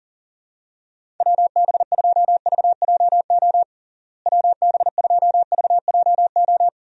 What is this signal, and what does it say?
Morse code, heard as a keyed tone, about 37 wpm, 700 Hz. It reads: WB2VJO WB2VJO
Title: USA Call Signs - 6 Characters - Repeated 2x - 40wpm